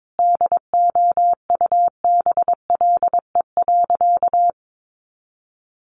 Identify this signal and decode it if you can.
Morse code, heard as a keyed tone, about 22 wpm, 700 Hz. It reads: DOUBLE.
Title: Sentences from Top 1400 Words 22wpm